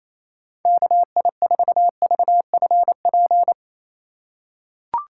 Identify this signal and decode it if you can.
Morse code, heard as a keyed tone, about 28 wpm, 700 Hz. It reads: KI4VFP
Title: USA Call Signs - 2x Character Spacing - 28wpm